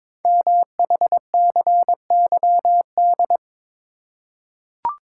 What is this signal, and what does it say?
Morse code, heard as a keyed tone, about 23 wpm, 700 Hz. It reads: MHCYD
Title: Code Groups - 5 Letters - 22wpm